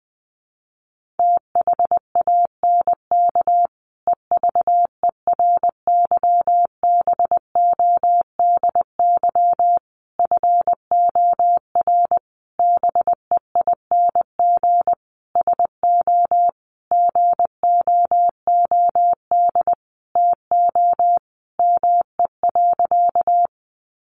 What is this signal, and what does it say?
Morse code, heard as a keyed tone, about 20 wpm, 700 Hz. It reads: THANK EVERYBODY FOR BEING SO GOOD TO ME.
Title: Sentences from Top 1400 Words 20wpm